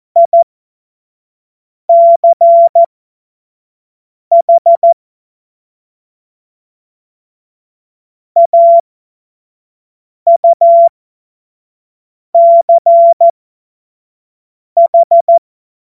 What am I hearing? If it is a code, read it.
Morse code, heard as a keyed tone, about 14 wpm, spaced as for 5 wpm, 680 Hz. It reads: ICH AUCH
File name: Morsekurs / Witze / Witz 11. Mai 2021, 25 F70